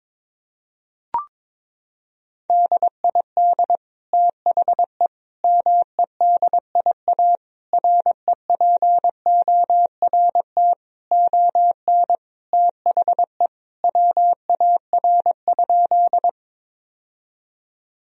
Morse code, heard as a keyed tone, about 21 wpm, 700 Hz. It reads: DID THE MEDIA REPORT ON THE WAR?
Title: Sentences from Top 1000 Words - Review Part 4 - 22wpm